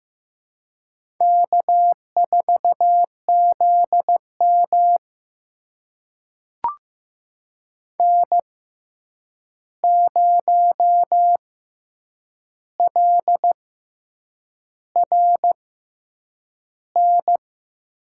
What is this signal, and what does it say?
Morse code, heard as a keyed tone, about 15 wpm, spaced as for 7 wpm, 700 Hz. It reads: K4ZM N0LRN
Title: USA Call Signs - 6x Character Spacing - 15wpm